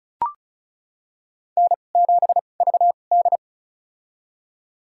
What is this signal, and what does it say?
Morse code, heard as a keyed tone, about 36 wpm, spaced as for 25 wpm, 700 Hz. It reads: N7VD
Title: USA Call Signs - 4 Characters - 2x Character Spacing - 35wpm